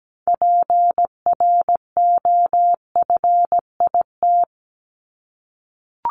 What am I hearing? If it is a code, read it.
Morse code, heard as a keyed tone, about 17 wpm, 700 Hz. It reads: PROFIT
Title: Top 751-1000 Words 17wpm